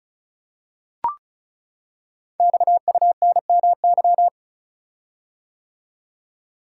Morse code, heard as a keyed tone, about 36 wpm, 700 Hz. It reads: XUNMY
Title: Code Groups - 5 Letters - 35wpm